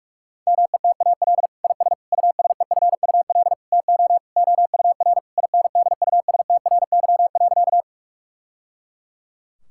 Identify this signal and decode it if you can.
Morse code, heard as a keyed tone, about 41 wpm, 700 Hz. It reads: METAL IS USEFUL TO OUR INDUSTRY.
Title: Sentences from Top 1000 Words - Review Part 6 - 45wpm